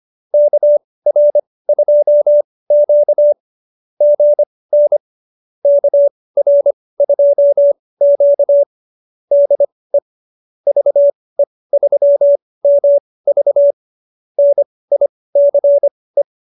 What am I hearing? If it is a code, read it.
Morse code, heard as a keyed tone, about 25 wpm, spaced as for 18 wpm, 590 Hz. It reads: KR2Q GN KR2Q DE VE3MV NICE